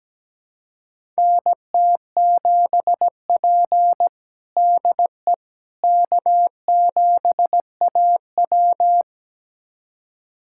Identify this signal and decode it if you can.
Morse code, heard as a keyed tone, about 17 wpm, 700 Hz. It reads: NT7P DE K7AW
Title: USA Callsign DE Callsign - 4 Characters - 17wpm